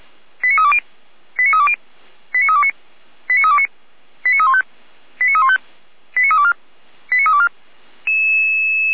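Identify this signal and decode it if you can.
Five-tone selective call (ZVEI), heard as quick runs of five tones, tones of about 70 ms. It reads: 89320 89320 89329 89329 89315 89315 89324 89324
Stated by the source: Five-tone ZVEI selective call received on SDR, from the Feuerwehr Orlamünde audio archive.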